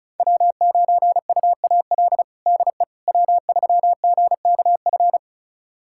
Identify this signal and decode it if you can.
Morse code, heard as a keyed tone, about 36 wpm, 690 Hz. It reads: W9UAL DE W3GKF
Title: USA Callsign DE Callsign - 5 Characters - 35wpm